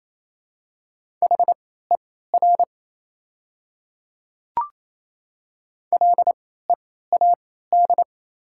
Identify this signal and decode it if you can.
Morse code, heard as a keyed tone, about 28 wpm, spaced as for 16 wpm, 700 Hz. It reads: HER LEAD
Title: 3x Character Spacing - Top 500 Words - 28wpm